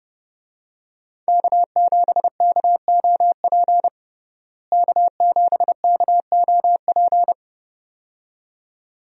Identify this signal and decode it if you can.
Morse code, heard as a keyed tone, about 30 wpm, 700 Hz. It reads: K7KOP K7KOP
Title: USA Call Signs - 5 Characters - Repeated 2x - 30wpm